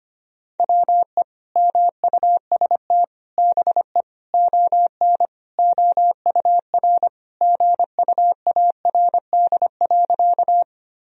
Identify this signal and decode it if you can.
Morse code, heard as a keyed tone, about 25 wpm, 700 Hz. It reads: WE MUST BE ON OUR GUARD.
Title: Sentences from Top 1000 Words - Review Part 4 - 25wpm